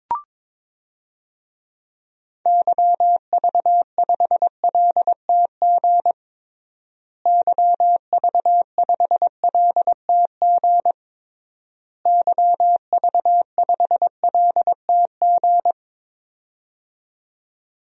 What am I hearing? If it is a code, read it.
Morse code, heard as a keyed tone, about 22 wpm, 700 Hz. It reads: YV5LTG YV5LTG YV5LTG
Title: DXCC-100 Call Signs - 6 Characters - Repeated 3x - 22wpm